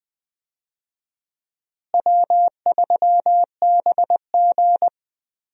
Morse code, heard as a keyed tone, about 21 wpm, 700 Hz. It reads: W3BG
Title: USA Call Signs - 4 Characters - Repeated 2x - 20wpm